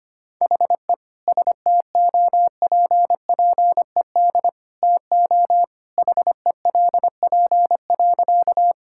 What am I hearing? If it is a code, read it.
Morse code, heard as a keyed tone, about 25 wpm, 700 Hz. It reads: HE STOPPED TO HELP.
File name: practice / 25wpm / Sentences with 2-Word Phrases - 25wpm